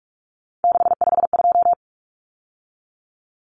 Morse code, heard as a keyed tone, about 44 wpm, 700 Hz. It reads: 652
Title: Numbers - 3-Digit 45wpm